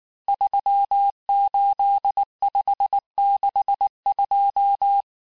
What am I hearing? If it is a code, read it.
Morse code, heard as a keyed tone, about 19 wpm, 800 Hz. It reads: 38562
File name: cours-du-lundi files / cours-1589-texte-chiffre-vitesse-18mpm